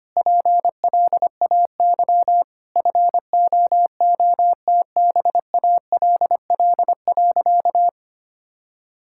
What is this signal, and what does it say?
Morse code, heard as a keyed tone, about 25 wpm, 700 Hz. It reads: PLAY FOOTBALL.
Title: Sentences from Top 1600 Words 25wpm